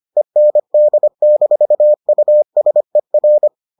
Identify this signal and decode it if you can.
Morse code, heard as a keyed tone, about 25 wpm, 600 Hz. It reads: END-USER